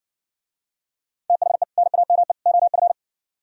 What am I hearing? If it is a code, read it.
Morse code, heard as a keyed tone, about 54 wpm, 700 Hz. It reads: THE NAME OF